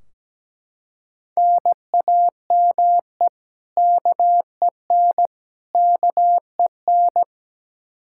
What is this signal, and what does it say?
Morse code, heard as a keyed tone, about 17 wpm, 700 Hz. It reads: NAME KEN KEN
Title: Practice QSO - Exchange 1 - Beginning - 17wpm